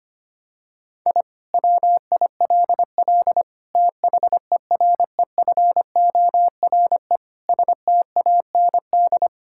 Morse code, heard as a keyed tone, about 25 wpm, 700 Hz. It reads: I WILL THEREFORE STAND
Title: Sentences from Top 1000 Words - Review Part 5 - 25wpm